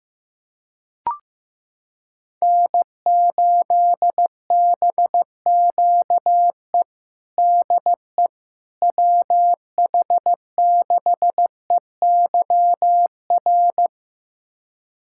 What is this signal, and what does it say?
Morse code, heard as a keyed tone, about 15 wpm, 700 Hz. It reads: N8BQE DE WH6EYR
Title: USA Callsign DE Callsign 15wpm